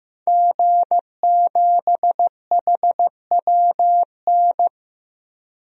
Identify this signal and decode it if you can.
Morse code, heard as a keyed tone, about 15 wpm, 700 Hz. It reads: G7HWN
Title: DXCC-100 Call Signs - 5 Characters - Repeated 2x - 15wpm